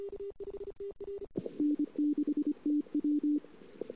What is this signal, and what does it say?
Morse code, heard as a keyed tone, about 25 wpm, 300 Hz. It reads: N6TW
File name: CW / 2008-ss-cw-k5tr-op-n5rz -22:00